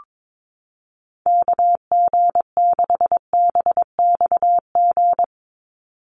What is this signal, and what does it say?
Morse code, heard as a keyed tone, about 22 wpm, 700 Hz. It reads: KG6BXG
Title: USA Call Signs 22wpm